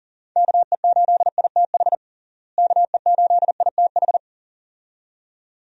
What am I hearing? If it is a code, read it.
Morse code, heard as a keyed tone, about 40 wpm, 700 Hz. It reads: KE8ITH KE8ITH
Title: USA Call Signs - 6 Characters - Repeated 2x - 40wpm